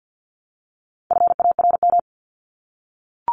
Morse code, heard as a keyed tone, about 50 wpm, 700 Hz. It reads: FARM
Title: Top 500 Words 50wpm